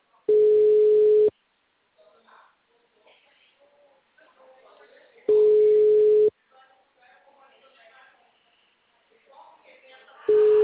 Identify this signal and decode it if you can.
Ringback tone (Europe, 425 Hz), with speech.
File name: out- -2012- - - .36250